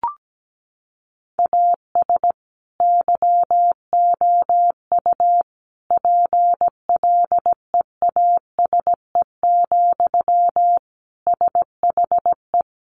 Morse code, heard as a keyed tone, about 17 wpm, 700 Hz. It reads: AS YOU PLEASE, SHE
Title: Sentences from Top 1500 Words 17wpm